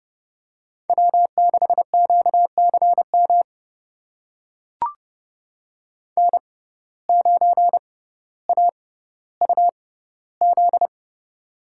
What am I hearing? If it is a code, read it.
Morse code, heard as a keyed tone, about 30 wpm, spaced as for 15 wpm, 700 Hz. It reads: W6QCM N9AUZ
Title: USA Call Signs - 5 Characters - 6x Character Spacing - 30wpm